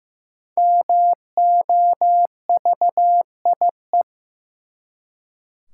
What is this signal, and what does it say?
Morse code, heard as a keyed tone, about 15 wpm, 700 Hz. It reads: MOVIE
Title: Five Letter Words - 15wpm